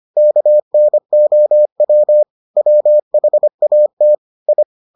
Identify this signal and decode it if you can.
Morse code, heard as a keyed tone, about 25 wpm, 600 Hz. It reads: KNOW WHAT I